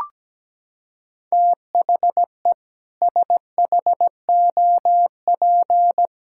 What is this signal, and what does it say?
Morse code, heard as a keyed tone, about 17 wpm, 700 Hz. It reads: THE SHOP
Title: Sentences from Top 1600 Words 17wpm